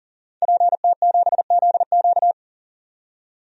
Morse code, heard as a keyed tone, about 40 wpm, 700 Hz. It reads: PT7ZQ
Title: DXCC-100 Call Signs - 5 Characters - 40wpm